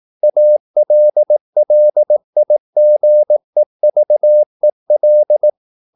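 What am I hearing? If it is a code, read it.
Morse code, heard as a keyed tone, about 18 wpm, 600 Hz. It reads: ALLIGEVEL
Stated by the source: Slow (QRS) Morse practice at 18 wpm, CW med Gnister.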